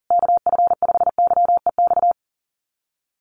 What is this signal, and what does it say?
Morse code, heard as a keyed tone, about 40 wpm, 700 Hz. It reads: KF5YEX
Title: USA Call Signs Repeated 3x 40wpm